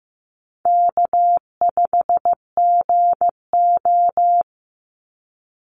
Morse code, heard as a keyed tone, about 15 wpm, 700 Hz. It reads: K5GO
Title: Easy Call Signs Repeated 2x - 15wpm